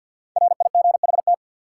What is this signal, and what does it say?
Morse code, heard as a keyed tone, about 53 wpm, 700 Hz. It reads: RIGHT
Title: Sets of 2 Words - New Word Drill - 50wpm